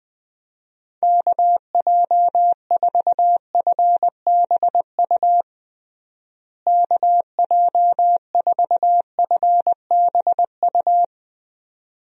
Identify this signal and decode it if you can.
Morse code, heard as a keyed tone, about 20 wpm, 700 Hz. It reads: KJ4FBU KJ4FBU
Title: USA Call Signs - 6 Characters - Repeated 2x - 20wpm